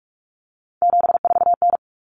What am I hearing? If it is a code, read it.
Morse code, heard as a keyed tone, about 44 wpm, 700 Hz. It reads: 74N
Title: Common Cut Numbers - 3-Digit 45wpm